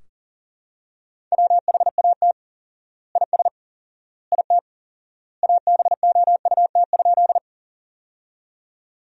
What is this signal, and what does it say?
Morse code, heard as a keyed tone, about 37 wpm, 700 Hz. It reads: WHAT IS IT ABOUT?
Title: 4x Word-Spacing - Sentences from Top 100 Words - 40wpm